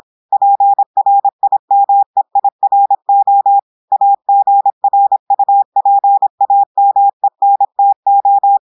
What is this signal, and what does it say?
Morse code, heard as a keyed tone, about 26 wpm, 800 Hz. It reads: PRIMEIRO AGRUPAMENTO
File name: audios / portugais / portugais texte en clair 1 vitesse 26 mots minutes